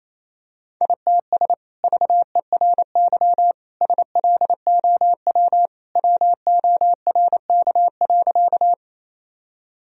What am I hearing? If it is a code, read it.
Morse code, heard as a keyed tone, about 28 wpm, 700 Hz. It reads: ITS VERY SLOW WORK.
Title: Sentences from Top 1000 Words - Review Part 2 - 28wpm